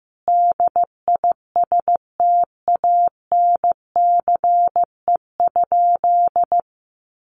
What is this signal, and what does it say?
Morse code, heard as a keyed tone, about 15 wpm, 700 Hz. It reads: DISTANCE?
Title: Sentences from Top 1200 Words 15wpm